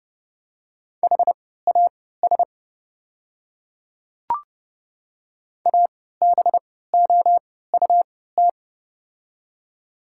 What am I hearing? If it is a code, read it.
Morse code, heard as a keyed tone, about 32 wpm, spaced as for 17 wpm, 700 Hz. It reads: HAS ABOUT
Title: 3x Character Spacing - Top 100 Words - 30wpm